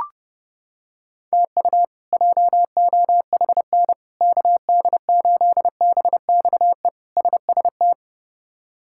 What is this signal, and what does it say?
Morse code, heard as a keyed tone, about 32 wpm, 700 Hz. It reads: TU JOHN KD8BXE SST
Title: K1USN SST QSO Practice - 30wpm